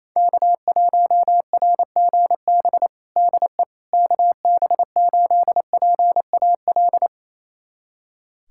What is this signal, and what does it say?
Morse code, heard as a keyed tone, about 27 wpm, 700 Hz. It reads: K1RGB DE KB8PAL
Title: USA Callsign DE Callsign 28wpm